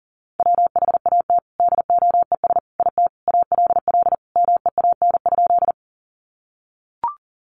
Morse code, heard as a keyed tone, about 40 wpm, 700 Hz. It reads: WHAT DOES IT ALL MEAN?
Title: Sentences from Top 200 Words 40wpm